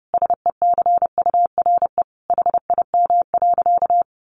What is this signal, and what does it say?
Morse code, heard as a keyed tone, about 30 wpm, 700 Hz. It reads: SECURE HIM.
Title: Sentences from Top 1200 Words 30wpm